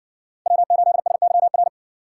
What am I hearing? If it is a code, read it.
Morse code, heard as a keyed tone, about 58 wpm, 700 Hz. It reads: W8SYR